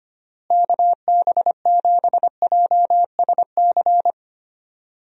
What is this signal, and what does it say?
Morse code, heard as a keyed tone, about 25 wpm, 700 Hz. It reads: KB7JSC
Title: USA Call Signs - 6 Characters - Repeated 2x - 25wpm